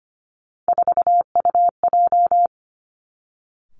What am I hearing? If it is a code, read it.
Morse code, heard as a keyed tone, about 25 wpm, 700 Hz. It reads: 4UJ
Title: Three Challenging Letter-Number Combinations 25wpm